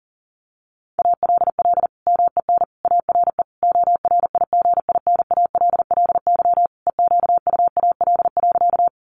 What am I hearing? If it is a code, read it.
Morse code, heard as a keyed tone, about 40 wpm, 700 Hz. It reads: ALL MEN ARE ORIGINALLY EQUAL.
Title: Sentences from Top 1500 Words 40wpm